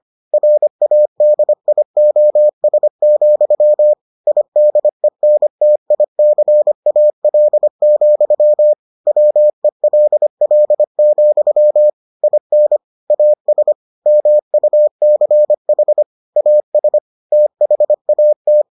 Morse code, heard as a keyed tone, about 25 wpm, 600 Hz. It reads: RADIOS, IDENTICAL, WELL, IN AS MUCH AS THAT